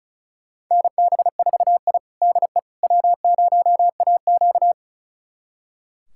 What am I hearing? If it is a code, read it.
Morse code, heard as a keyed tone, about 34 wpm, 700 Hz. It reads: NB4I DE W0AQ
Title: USA Callsign DE Callsign - 4 Characters - 35wpm